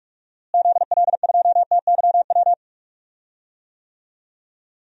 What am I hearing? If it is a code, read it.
Morse code, heard as a keyed tone, about 43 wpm, 700 Hz. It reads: ZL2TYW
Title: DXCC-100 Call Signs - 6 Characters - Repeated 2x - 45wpm